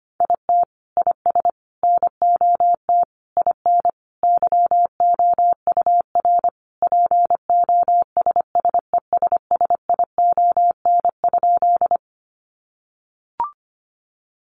Morse code, heard as a keyed tone, about 25 wpm, 700 Hz. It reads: IT IS NOT IN YOUR POSSESSION?
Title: Sentences from Top 1400 Words 25wpm